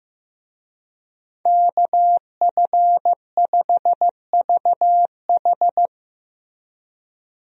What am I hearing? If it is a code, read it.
Morse code, heard as a keyed tone, about 15 wpm, 700 Hz. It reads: KF5VH
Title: USA Call Signs - 5 Characters - Repeated 3x - 15wpm